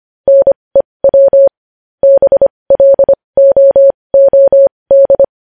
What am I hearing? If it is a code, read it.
Morse code, heard as a keyed tone, about 25 wpm, spaced as for 20 wpm, 550 Hz. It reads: NEW BLOOD
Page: News Headlines in Morse Code at 20 WPM